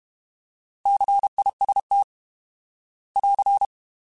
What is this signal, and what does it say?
Morse code, heard as a keyed tone, about 31 wpm, 800 Hz. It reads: CIST <AR>
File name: Alphabetical / Alphabetical mp3s / Cs @30